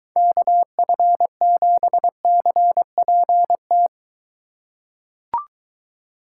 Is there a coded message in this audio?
Yes: Morse code from 0.2 to 3.9 s, heard as a keyed tone, about 23 wpm, 700 Hz. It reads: KF7CPT